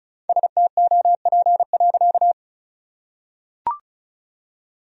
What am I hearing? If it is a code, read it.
Morse code, heard as a keyed tone, about 36 wpm, 700 Hz. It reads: STOP.
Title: Sentences from Top 1000 Words - Review Part 1 - 35wpm